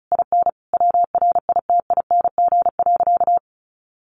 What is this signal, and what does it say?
Morse code, heard as a keyed tone, about 34 wpm, 700 Hz. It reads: IN WRITING.
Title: Sentences from Top 900 Words 35wpm